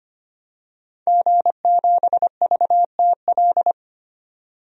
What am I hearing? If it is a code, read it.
Morse code, heard as a keyed tone, about 25 wpm, 700 Hz. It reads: G7VTL
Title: DXCC-100 Call Signs - 5 Characters - 25wpm